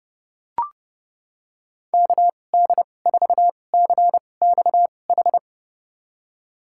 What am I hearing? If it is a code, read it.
Morse code, heard as a keyed tone, about 30 wpm, spaced as for 22 wpm, 700 Hz. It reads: KD4CXH
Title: USA Call Signs - 6 Characters - 2x Character Spacing - 30wpm